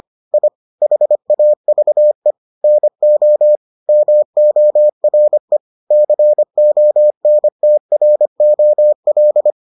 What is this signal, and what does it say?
Morse code, heard as a keyed tone, about 25 wpm, 600 Hz. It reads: I HAVE NO MORE CONTROL